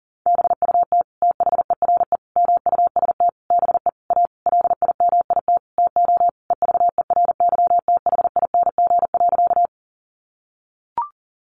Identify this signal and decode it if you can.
Morse code, heard as a keyed tone, about 37 wpm, 700 Hz. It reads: BUT THERE MUST BE A LIMIT TO EVERYTHING.
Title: Sentences from Top 1500 Words 40wpm